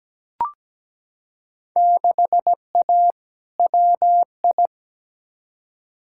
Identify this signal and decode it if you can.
Morse code, heard as a keyed tone, about 17 wpm, 700 Hz. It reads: 6A WI
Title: Field Day Exchanges - 17wpm